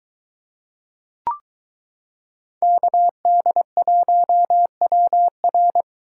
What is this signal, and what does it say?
Morse code, heard as a keyed tone, about 23 wpm, 700 Hz. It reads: KD1WR